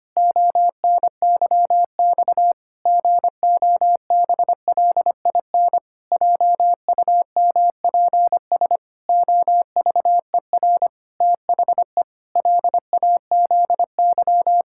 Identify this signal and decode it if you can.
Morse code, heard as a keyed tone, about 25 wpm, 700 Hz. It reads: ONYX GOBLIN JUMPS OVER THE LAZY
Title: Pangram Challenge v2 - 25wpm